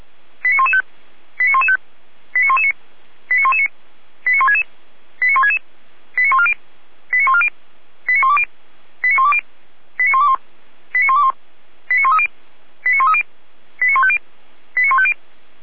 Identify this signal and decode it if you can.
Five-tone selective call (ZVEI), heard as quick runs of five tones, tones of about 80 ms. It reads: 89115 89115 89119 89119 89166 89166 89140 89140 89120 89120 89121 89121 89130 89130 89150 89150
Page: ZVEI